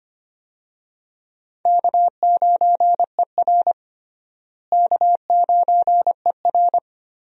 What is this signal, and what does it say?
Morse code, heard as a keyed tone, about 25 wpm, 700 Hz. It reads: K9ER K9ER
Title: USA Call Signs - 4 Characters - Repeated 3x - 25wpm